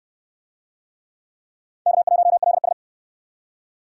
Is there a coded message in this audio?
Yes: Morse code from 1.9 to 2.7 s, heard as a keyed tone, about 68 wpm, 700 Hz. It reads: K1LS